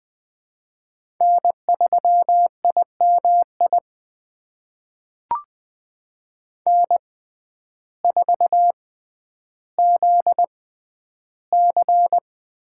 Morse code, heard as a keyed tone, about 20 wpm, spaced as for 11 wpm, 700 Hz. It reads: N3IMI N4ZC
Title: USA Call Signs - 5 Characters - 6x Character Spacing - 20wpm